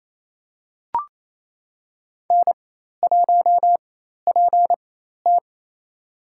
Morse code, heard as a keyed tone, about 28 wpm, spaced as for 13 wpm, 700 Hz. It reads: N1PT
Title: USA Call Signs - 4 Characters - 4x Character Spacing - 28wpm